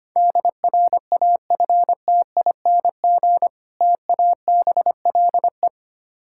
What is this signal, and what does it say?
Morse code, heard as a keyed tone, about 25 wpm, 690 Hz. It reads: DRAFTING TABLE
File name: practice / 25wpm / Difficult Words with Hints - 25wpm